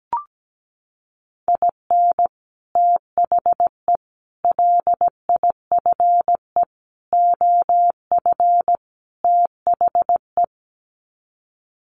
Morse code, heard as a keyed tone, about 17 wpm, 700 Hz. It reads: IN THE LIFE OF THE
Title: Sets of 5 Words Encore2 17wpm